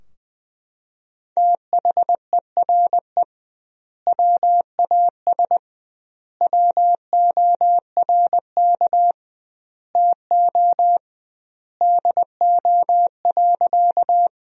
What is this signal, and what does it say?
Morse code, heard as a keyed tone, about 20 wpm, 700 Hz. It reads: THERE WAS WORK TO DO.
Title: 2x Word-Spacing - Sentences from Top 200 Words - 20wpm